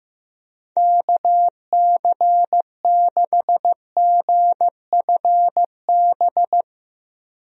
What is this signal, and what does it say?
Morse code, heard as a keyed tone, about 15 wpm, 700 Hz. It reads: KC6GFB
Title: USA Call Signs - 6 Characters - 15wpm